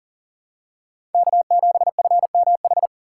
Morse code, heard as a keyed tone, about 40 wpm, 700 Hz. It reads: K7FMH
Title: USA Call Signs - 5 Characters - 3x Character Spacing - 40wpm